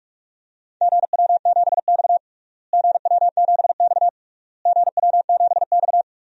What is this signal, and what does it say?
Morse code, heard as a keyed tone, about 46 wpm, 700 Hz. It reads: GW7X GW7X GW7X
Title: DXCC-100 Call Signs - 4 Characters - Repeated 3x - 45wpm